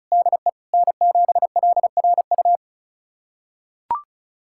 Morse code, heard as a keyed tone, about 36 wpm, 700 Hz. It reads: DE N7LRU
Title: USA Callsign DE Callsign 35wpm